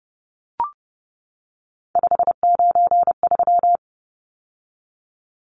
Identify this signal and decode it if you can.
Morse code, heard as a keyed tone, about 30 wpm, 700 Hz. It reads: 593
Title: Numbers - 3-Digit 30wpm